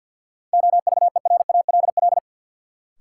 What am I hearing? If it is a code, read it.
Morse code, heard as a keyed tone, about 53 wpm, 700 Hz. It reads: OVERALL
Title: Sentences from Top 1000 Words - Review Part 7 - New Word Drill - 50wpm